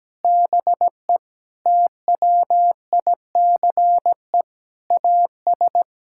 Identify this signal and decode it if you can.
Morse code, heard as a keyed tone, about 17 wpm, 700 Hz. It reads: BE TWICE AS
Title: Sentences from Top 1600 Words 17wpm